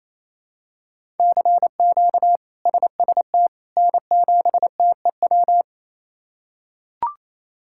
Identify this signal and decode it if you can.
Morse code, heard as a keyed tone, about 28 wpm, 700 Hz. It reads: CQ SST N7TEW
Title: K1USN SST QSO Practice - 28wpm